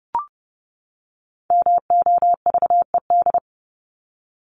Morse code, heard as a keyed tone, about 30 wpm, 700 Hz. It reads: MOVED
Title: ED Suffix - Top 100 Words with Base Words 30wpm